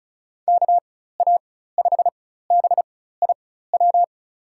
Morse code, heard as a keyed tone, about 34 wpm, spaced as for 16 wpm, 700 Hz. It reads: KA5BIW
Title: USA Call Signs - 6 Characters - 4x Character Spacing - 35wpm